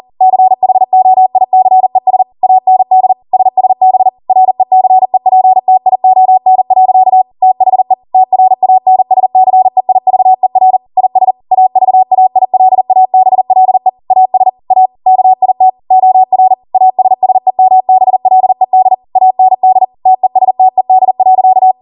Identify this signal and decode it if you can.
Morse code, heard as a keyed tone, about 40 wpm, 750 Hz. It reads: CHOICES AND SSB RECEPTION. THE TRANSCEIVER IS AVAILABLE AS A KIT OR ASSEMBLED AND TESTED.
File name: Morse / 40WPM